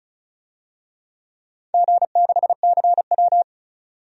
Morse code, heard as a keyed tone, about 36 wpm, 690 Hz. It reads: G6CW
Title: DXCC-100 Call Signs - 4 Characters - 35wpm